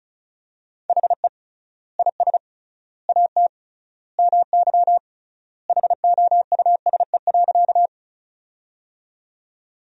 Morse code, heard as a keyed tone, about 34 wpm, 700 Hz. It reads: HE IS AT MY HOUSE.
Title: 3x Word-Spacing - Sentences from Top 200 Words - 35wpm